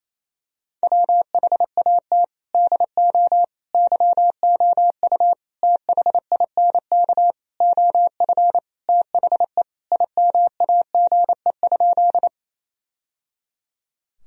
Morse code, heard as a keyed tone, about 28 wpm, 700 Hz. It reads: WHAT DO YOU THINK OF THE IMAGE?
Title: Sentences from Top 1000 Words - Review Part 4 - 28wpm